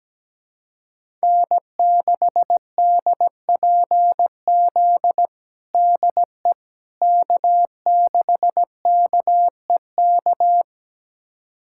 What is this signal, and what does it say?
Morse code, heard as a keyed tone, about 17 wpm, 700 Hz. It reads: N6DPZ DE K6KEK
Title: USA Callsign DE Callsign - 5 Characters - 17wpm